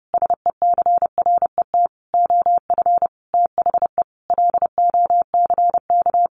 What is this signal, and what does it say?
Morse code, heard as a keyed tone, about 30 wpm, 700 Hz. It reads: SECRET OF THE LOCK
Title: Sentences from Top 1300 Words 30wpm